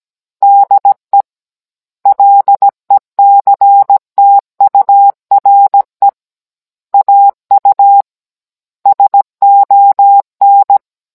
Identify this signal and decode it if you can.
Morse code, heard as a keyed tone, about 17 wpm, 800 Hz. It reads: DE LECTURE AU SON
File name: Cours-NR19-16mm